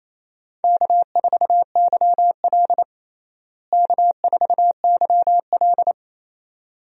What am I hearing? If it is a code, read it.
Morse code, heard as a keyed tone, about 28 wpm, 700 Hz. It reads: K4YL K4YL
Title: USA Call Signs - 4 Characters - Repeated 2x - 28wpm